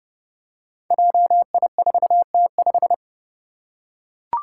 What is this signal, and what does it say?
Morse code, heard as a keyed tone, about 30 wpm, 700 Hz. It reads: JI4T5
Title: Five Challenging Letter-Number Combinations 30wpm